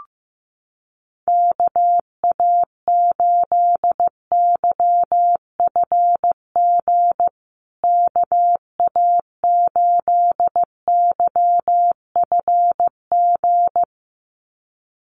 Morse code, heard as a keyed tone, about 15 wpm, 700 Hz. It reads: KA8YFG KA8YFG
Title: USA Call Signs Repeated 2x 15wpm